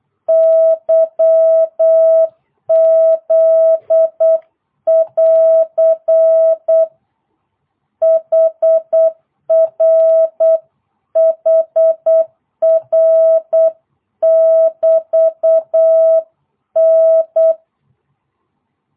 Morse code, heard as a keyed tone, about 8 wpm, 650 Hz. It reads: YZ <AR> HR HR <BT> N